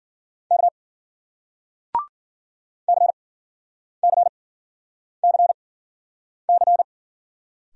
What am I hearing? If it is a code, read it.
Morse code, heard as a keyed tone, about 51 wpm, spaced as for 9 wpm, 700 Hz. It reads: K CCCC